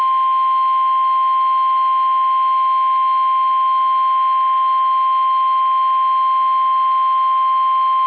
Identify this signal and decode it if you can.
Emergency Alert System (SAME) tones (the 1050 Hz attention tone). It is heard from 1.0 to 7.3 s.
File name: eas / KUSO / KUSO